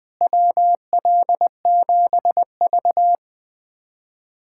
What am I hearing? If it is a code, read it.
Morse code, heard as a keyed tone, about 20 wpm, 700 Hz. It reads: WL7V
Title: USA Callsign DE Callsign - 4 Characters - 20wpm